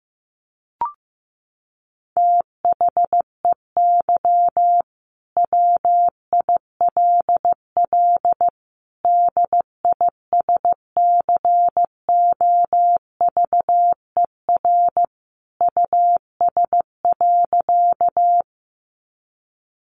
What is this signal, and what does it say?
Morse code, heard as a keyed tone, about 15 wpm, 700 Hz. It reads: THEY WILL DISCOVER US.
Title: Sentences from Top 1100 Words 15wpm